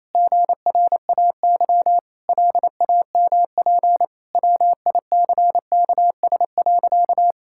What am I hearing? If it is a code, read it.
Morse code, heard as a keyed tone, about 27 wpm, 700 Hz. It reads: GRAY LAMP WICKS.
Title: Pangram Challenge v2 - 28wpm